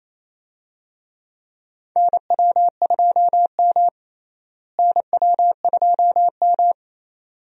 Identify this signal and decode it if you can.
Morse code, heard as a keyed tone, about 28 wpm, 700 Hz. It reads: NW2M NW2M
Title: USA Call Signs - 4 Characters - Repeated 2x - 28wpm